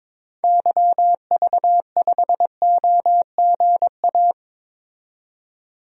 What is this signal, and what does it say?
Morse code, heard as a keyed tone, about 22 wpm, 700 Hz. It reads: YV5OGA
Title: DXCC-100 Call Signs - 6 Characters - 22wpm